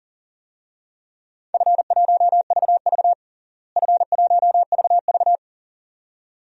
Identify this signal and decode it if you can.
Morse code, heard as a keyed tone, about 40 wpm, 700 Hz. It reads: F1VV F1VV
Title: DXCC-100 Call Signs - 4 Characters - Repeated 2x - 40wpm